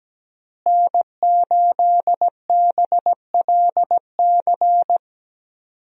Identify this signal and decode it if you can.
Morse code, heard as a keyed tone, about 17 wpm, 700 Hz. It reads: N8BLC
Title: USA Call Signs - 5 Characters - Repeated 2x - 17wpm